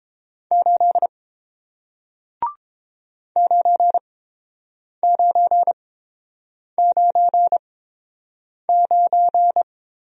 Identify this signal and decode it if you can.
Morse code, heard as a keyed tone, about 30 wpm, spaced as for 7 wpm, 700 Hz. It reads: 8 9999